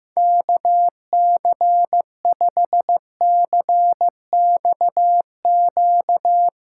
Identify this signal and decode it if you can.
Morse code, heard as a keyed tone, about 15 wpm, 700 Hz. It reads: KC5CXQ
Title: USA Callsign DE Callsign 15wpm